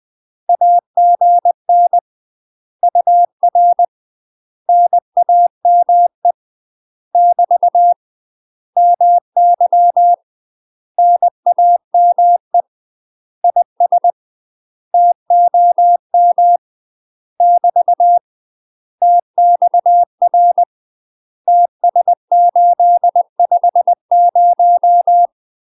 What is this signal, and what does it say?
Morse code, heard as a keyed tone, about 20 wpm, 700 Hz. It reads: AGN UR NAME <BT> MY NAME IS TOM <BT> TXR TS850
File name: UK MASTER QSOs TRAINING / 20WPM / UK MASTER QSOs 0057